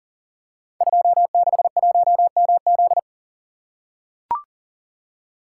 Morse code, heard as a keyed tone, about 40 wpm, 700 Hz. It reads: 261MZ
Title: Code Groups - 5 Letter:Numbers - 40wpm